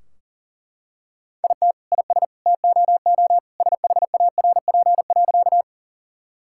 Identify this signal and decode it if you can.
Morse code, heard as a keyed tone, about 40 wpm, 700 Hz. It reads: IT IS TOO SHARP.
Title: Sentences from Top 1600 Words 40wpm